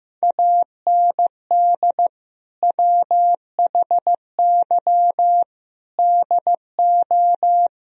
Morse code, heard as a keyed tone, about 15 wpm, 700 Hz. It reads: AND WHY DO
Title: Sentences from Top 1000 Words - Review Part 4 - 15wpm